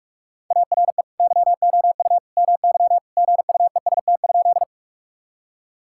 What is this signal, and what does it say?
Morse code, heard as a keyed tone, about 46 wpm, 700 Hz. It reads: ARE YOU MY GUEST?
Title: Sentences from Top 1600 Words 45wpm